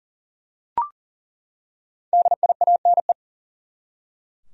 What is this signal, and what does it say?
Morse code, heard as a keyed tone, about 38 wpm, 700 Hz. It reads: DIANE
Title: USA Common Names - 40wpm